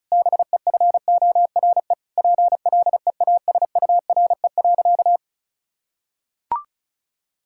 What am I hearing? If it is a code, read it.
Morse code, heard as a keyed tone, about 36 wpm, 700 Hz. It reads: BEFORE PLEASURE.
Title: Sentences from Top 1600 Words 35wpm